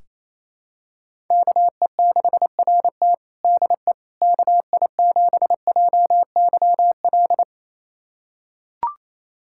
Morse code, heard as a keyed tone, about 28 wpm, 700 Hz. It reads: KE6RT DE KI7JYL
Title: USA Callsign DE Callsign 28wpm